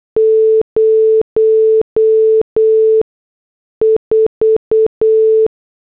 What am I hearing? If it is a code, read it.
Morse code, heard as a keyed tone, about 8 wpm, spaced as for 6 wpm, 440 Hz. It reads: 04
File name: morse / headlines